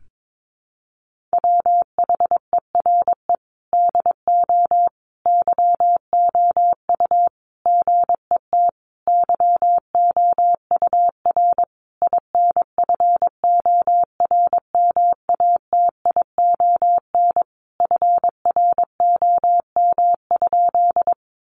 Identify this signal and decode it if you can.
Morse code, heard as a keyed tone, about 22 wpm, 700 Hz. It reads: WHERE DO YOU GET YOUR INFORMATION FROM?
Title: Sentences from Top 600 Words 22wpm